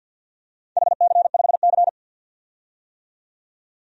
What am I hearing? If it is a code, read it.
Morse code, heard as a keyed tone, about 51 wpm, 700 Hz. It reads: HC5C
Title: DXCC-100 Call Signs - 4 Characters - 50wpm